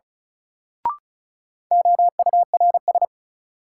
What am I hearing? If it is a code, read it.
Morse code, heard as a keyed tone, about 34 wpm, 700 Hz. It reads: OURS
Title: Sentences from Top 1300 Words - New Word Drill Alt - 35wpm